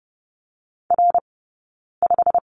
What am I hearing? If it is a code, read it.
Morse code, heard as a keyed tone, about 30 wpm, spaced as for 9 wpm, 700 Hz. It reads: R5
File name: Lesson 18d - Letter m - Two Characters 30wpm